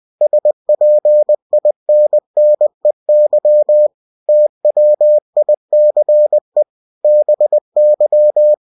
Morse code, heard as a keyed tone, about 20 wpm, 600 Hz. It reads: SPINNEY TWICE BY
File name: cw-books / winnie-the-pooh / winnie-20wpm-page-0025